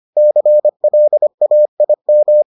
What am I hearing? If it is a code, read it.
Morse code, heard as a keyed tone, about 25 wpm, 600 Hz. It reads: CLAIM